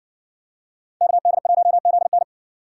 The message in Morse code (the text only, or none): KD1ZN